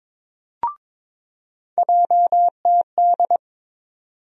Morse code, heard as a keyed tone, about 22 wpm, 700 Hz. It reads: JTD
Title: USA Call Sign Postfixes - 3 Letters 22wpm